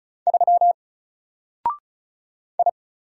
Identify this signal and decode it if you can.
Morse code, heard as a keyed tone, about 36 wpm, 700 Hz. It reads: 3 I
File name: Single-Letter-Number-35wpm